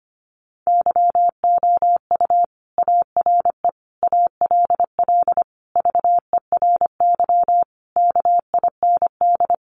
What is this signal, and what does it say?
Morse code, heard as a keyed tone, about 25 wpm, 700 Hz. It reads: YOU ARE ALL VERY KIND
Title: Sentences from Top 200 Words 25wpm